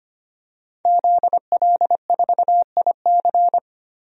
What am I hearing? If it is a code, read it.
Morse code, heard as a keyed tone, about 25 wpm, 700 Hz. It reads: ZL4IC
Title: DXCC-100 Call Signs - 5 Characters - 25wpm